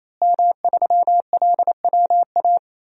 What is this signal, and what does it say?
Morse code, heard as a keyed tone, about 28 wpm, 700 Hz. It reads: M3LWA
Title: DXCC-100 Call Signs - 5 Characters - Repeated 3x - 28wpm